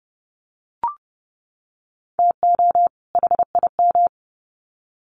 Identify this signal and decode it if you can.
Morse code, heard as a keyed tone, about 30 wpm, 700 Hz. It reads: TO HIM
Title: Sets of 2 Words 30wpm